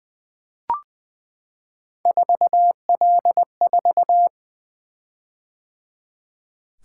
Morse code, heard as a keyed tone, about 20 wpm, 700 Hz. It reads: 4L4
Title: Three Letter-Number Combinations - 20wpm